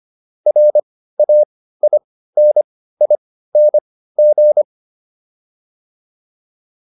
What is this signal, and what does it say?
Morse code, heard as a keyed tone, about 25 wpm, spaced as for 15 wpm, 600 Hz. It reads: RAINING
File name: QSO402 15